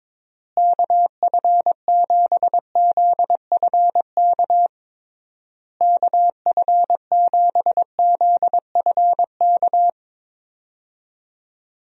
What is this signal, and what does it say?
Morse code, heard as a keyed tone, about 22 wpm, 700 Hz. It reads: KF7ZFK KF7ZFK
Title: USA Call Signs - 6 Characters - Repeated 3x - 22wpm